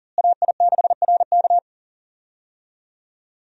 Morse code, heard as a keyed tone, about 40 wpm, 700 Hz. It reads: AI6RK
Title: USA Call Signs - 5 Characters - Repeated 3x - 40wpm